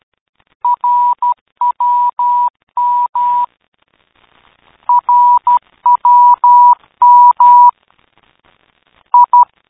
Morse code, heard as a keyed tone, about 12 wpm, 970 Hz. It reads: RWM RWM I
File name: Time / Zvoki / rwm